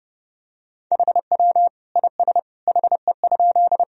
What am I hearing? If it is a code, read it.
Morse code, heard as a keyed tone, about 30 wpm, 700 Hz. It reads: HW IS HE?
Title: Sentences from Top 100 Words and Abbreviations 30wpm